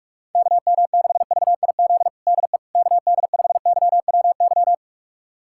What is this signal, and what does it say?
Morse code, heard as a keyed tone, about 41 wpm, 700 Hz. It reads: KM6VIZ DE KD5YWY
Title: USA Callsign DE Callsign - 6 Characters - 45wpm